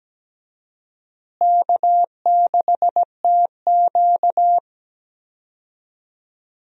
Morse code, heard as a keyed tone, about 17 wpm, 700 Hz. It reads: K6TQ
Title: USA Call Signs - 4 Characters - Repeated 3x - 17wpm